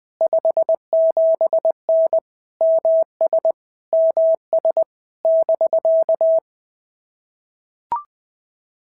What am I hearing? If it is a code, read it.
Morse code, heard as a keyed tone, about 20 wpm, 650 Hz. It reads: 57N MS MS <BK>
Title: POTA QSO Practice - 20wpm